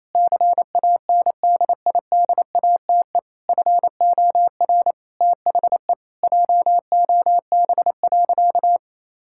Morse code, heard as a keyed tone, about 28 wpm, 700 Hz. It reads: CANDIDATE FOR THE JOB.
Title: Sentences from Top 1600 Words 28wpm